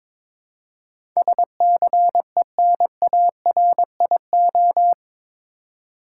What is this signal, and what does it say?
Morse code, heard as a keyed tone, about 22 wpm, 700 Hz. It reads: SCENARIO
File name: practice / 22wpm / Eight Letter Words - 22wpm